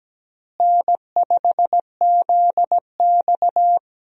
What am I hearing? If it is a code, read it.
Morse code, heard as a keyed tone, about 17 wpm, 700 Hz. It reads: N5ZX
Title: USA Call Signs - 4 Characters - Repeated 3x - 17wpm